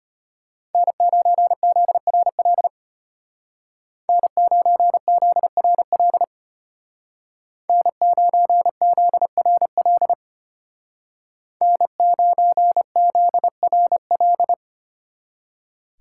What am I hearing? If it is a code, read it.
Morse code, heard as a keyed tone, about 30 wpm, 700 Hz. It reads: N9ZRL N9ZRL N9ZRL N9ZRL
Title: Speed Racing - USA 1x3 Call Signs 25wpm